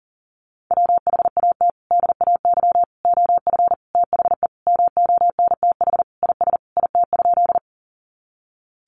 Morse code, heard as a keyed tone, about 40 wpm, 690 Hz. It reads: WHAT DAY OF THE MONTH IS IT?
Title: Sentences from Top 800 Words 40wpm